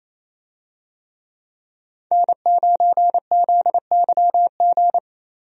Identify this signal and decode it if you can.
Morse code, heard as a keyed tone, about 28 wpm, 700 Hz. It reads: N9ZYG
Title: USA Call Signs - 5 Characters - 28wpm